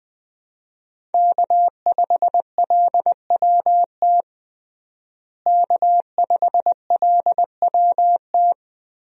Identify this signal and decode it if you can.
Morse code, heard as a keyed tone, about 20 wpm, 700 Hz. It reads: K5LWT K5LWT
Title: USA Call Signs - 5 Characters - Repeated 3x - 20wpm